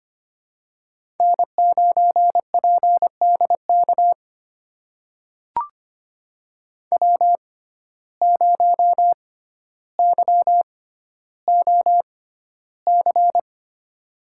Morse code, heard as a keyed tone, about 25 wpm, spaced as for 13 wpm, 700 Hz. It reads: N9PDK W0YOC
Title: USA Call Signs - 6x Character Spacing - 25wpm